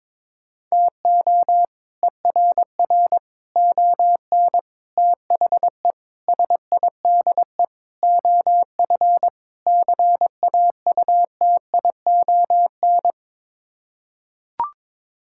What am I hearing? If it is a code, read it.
Morse code, heard as a keyed tone, about 22 wpm, 700 Hz. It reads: TO ERR ON THE SIDE OF CAUTION
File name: practice / 22wpm / Sets of 7 Words - 22wpm